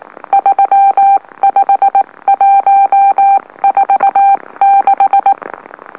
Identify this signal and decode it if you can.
Morse code, heard as a keyed tone, about 19 wpm, spaced as for 16 wpm, 790 Hz. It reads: 35146
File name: M04- -2345utc-LOLO